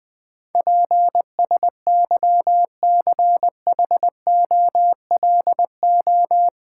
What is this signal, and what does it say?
Morse code, heard as a keyed tone, about 20 wpm, 700 Hz. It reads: PSYCHOLO
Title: Build A Word - 4 Letters At A Time - 12-Letter Words - 20wpm